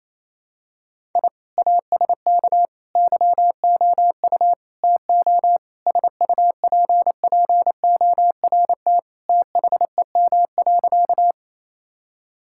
Morse code, heard as a keyed tone, about 28 wpm, 700 Hz. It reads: I ASK YOU TO SUPPORT THEM.
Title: Sentences from Top 1000 Words - Review Part 5 - 28wpm